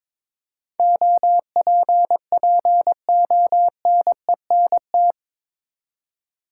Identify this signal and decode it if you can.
Morse code, heard as a keyed tone, about 21 wpm, 700 Hz. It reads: OPPONENT
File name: practice / 22wpm / Eight Letter Words - 22wpm